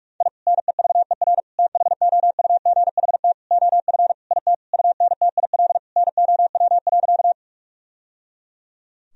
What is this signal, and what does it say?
Morse code, heard as a keyed tone, about 46 wpm, 700 Hz. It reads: I NEVER THOUGHT OF IT UNTIL NOW.
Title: Sentences from Top 1000 Words - Review Part 1 - 45wpm